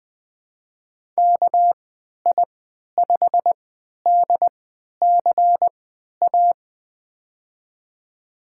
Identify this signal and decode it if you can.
Morse code, heard as a keyed tone, about 20 wpm, spaced as for 11 wpm, 700 Hz. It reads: KI5DCA
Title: USA Call Signs - 6 Characters - 3x Character Spacing - 20wpm